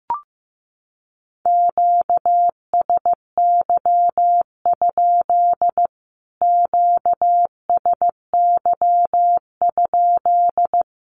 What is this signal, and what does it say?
Morse code, heard as a keyed tone, about 15 wpm, 700 Hz. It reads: QSY? QSY?
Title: Q-Codes Repeated 3x 15wpm